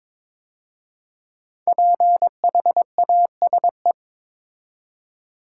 Morse code, heard as a keyed tone, about 22 wpm, 700 Hz. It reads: PHASE
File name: 5 Letter Words - First 3 Letters Given - 22wpm